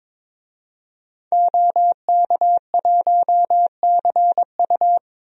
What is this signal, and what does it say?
Morse code, heard as a keyed tone, about 22 wpm, 700 Hz. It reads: OK1CU
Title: DXCC-100 Call Signs - 5 Characters - Repeated 2x - 22wpm